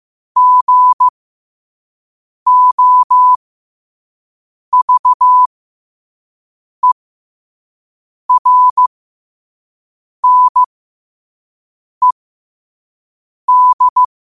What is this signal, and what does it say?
Morse code, heard as a keyed tone, about 15 wpm, spaced as for 5 wpm, 1000 Hz. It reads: GOVERNED
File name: qotdfiles / 5wpm